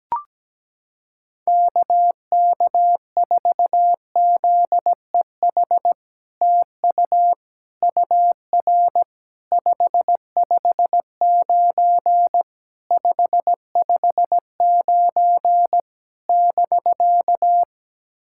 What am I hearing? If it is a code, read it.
Morse code, heard as a keyed tone, about 17 wpm, 700 Hz. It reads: KK4ZEH TU UR 559 559 <BK>
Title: SOTA QSO Practice 17wpm